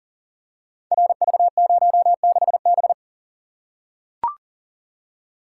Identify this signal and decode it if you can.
Morse code, heard as a keyed tone, about 41 wpm, 700 Hz. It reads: RV06B
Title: Code Groups - 5 Letter:Numbers - 40wpm